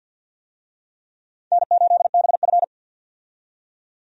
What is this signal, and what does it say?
Morse code, heard as a keyed tone, about 53 wpm, 700 Hz. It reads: N8BF
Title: USA Call Signs - 4 Characters - 50wpm